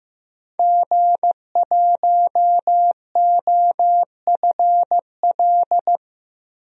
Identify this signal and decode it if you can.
Morse code, heard as a keyed tone, about 16 wpm, 700 Hz. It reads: G1OFL
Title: DXCC-100 Call Signs - 5 Characters - 15wpm